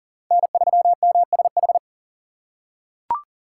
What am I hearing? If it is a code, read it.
Morse code, heard as a keyed tone, about 40 wpm, 700 Hz. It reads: N3MSH
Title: USA Call Signs - 3x Character Spacing - 40wpm